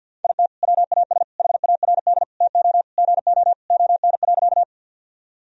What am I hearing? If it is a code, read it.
Morse code, heard as a keyed tone, about 52 wpm, 700 Hz. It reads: IT WAS HARD TO GO ON.
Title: Sentences from Top 1000 Words - Review Part 1 - 50wpm